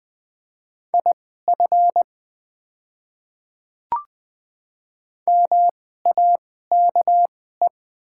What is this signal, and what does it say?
Morse code, heard as a keyed tone, about 20 wpm, spaced as for 14 wpm, 700 Hz. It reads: IF MAKE
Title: 2x Character Spacing - Top 100 Words - 20wpm